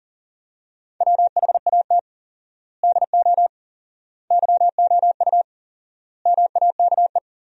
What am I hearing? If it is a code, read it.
Morse code, heard as a keyed tone, about 40 wpm, 700 Hz. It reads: WHAT DO YOU MAKE
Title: 4x Word-Spacing - Sentences from Top 100 Words - 40wpm